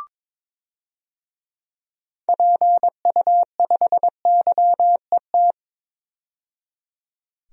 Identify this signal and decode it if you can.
Morse code, heard as a keyed tone, about 22 wpm, 700 Hz. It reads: PU5YET
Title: DXCC-100 Call Signs - 6 Characters - 22wpm